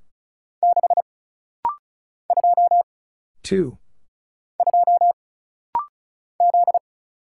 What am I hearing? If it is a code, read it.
Morse code, heard as a keyed tone, about 36 wpm, 700 Hz. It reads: 6 2 2 Z
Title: Single Letter-Number Alt - Rapid-Fire 35wpm